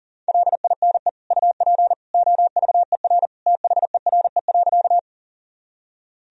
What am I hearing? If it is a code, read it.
Morse code, heard as a keyed tone, about 40 wpm, 700 Hz. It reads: LINE UP OVER THERE.
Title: Sentences from Top 1000 Words - Review Part 1 - 40wpm